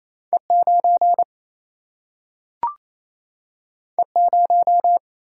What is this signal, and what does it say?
Morse code, heard as a keyed tone, about 28 wpm, 700 Hz. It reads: E 9 E 0
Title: Single-Letter Single-Number 28wpm